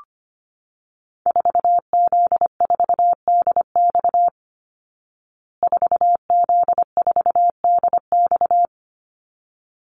Morse code, heard as a keyed tone, about 25 wpm, 700 Hz. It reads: 4Z4DX 4Z4DX
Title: Call Signs Repeated 3x - 25wpm